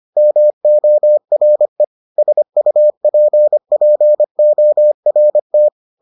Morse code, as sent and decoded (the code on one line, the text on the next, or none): -- --- .-. / ... ..- .--. .--. --- .-. -
MORE SUPPORT